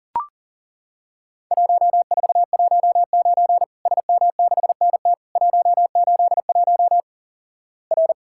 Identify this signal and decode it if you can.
Morse code, heard as a keyed tone, about 37 wpm, 700 Hz. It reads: 1419 SM6NT 181 R
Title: Worked All Europe DX Contest - QTC Exchange - 40wpm